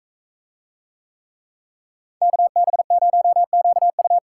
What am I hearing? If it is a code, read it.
Morse code, heard as a keyed tone, about 44 wpm, 700 Hz. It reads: KB0QU